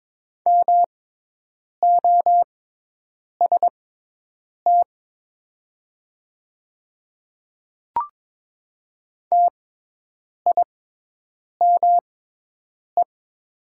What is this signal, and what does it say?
Morse code, heard as a keyed tone, about 22 wpm, spaced as for 8 wpm, 700 Hz. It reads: MOST TIME
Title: 6x Character Spacing - Top 100 Words - 22wpm